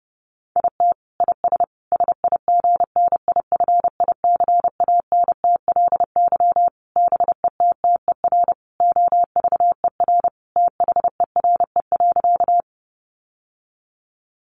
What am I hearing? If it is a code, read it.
Morse code, heard as a keyed tone, about 30 wpm, 700 Hz. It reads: IT IS SIGNIFICANTLY BETTER OVER THERE.
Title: Sentences from Top 1500 Words 30wpm